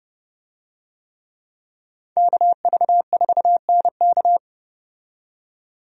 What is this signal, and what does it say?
Morse code, heard as a keyed tone, about 30 wpm, 700 Hz. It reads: KV4NK
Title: USA Call Signs - 5 Characters - 30wpm